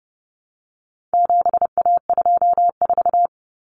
Morse code, heard as a keyed tone, about 30 wpm, 700 Hz. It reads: 7A24
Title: Common Cut Numbers - 4-Digit 30wpm